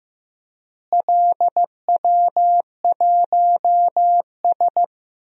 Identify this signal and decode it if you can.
Morse code, heard as a keyed tone, about 15 wpm, 700 Hz. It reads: LW1S
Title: DXCC-100 Call Signs - 4 Characters - Repeated 2x - 15wpm